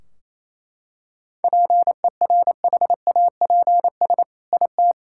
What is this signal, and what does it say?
Morse code, heard as a keyed tone, about 28 wpm, 700 Hz. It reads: PERHAPS IT
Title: Sentences from Top 1600 Words 28wpm